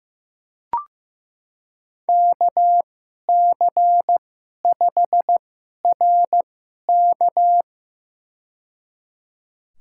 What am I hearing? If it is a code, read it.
Morse code, heard as a keyed tone, about 15 wpm, spaced as for 11 wpm, 700 Hz. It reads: KC5RK
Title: USA Call Signs - 5 Characters - 2x Character Spacing - 15wpm